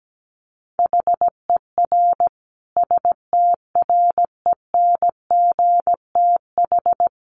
Morse code, heard as a keyed tone, about 17 wpm, 700 Hz. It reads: HER STRENGTH
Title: Sentences from Top 1300 Words 17wpm